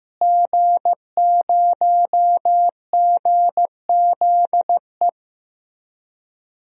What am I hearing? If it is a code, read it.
Morse code, heard as a keyed tone, about 15 wpm, 700 Hz. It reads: G0GZE
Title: DXCC-100 Call Signs - 5 Characters - Repeated 3x - 15wpm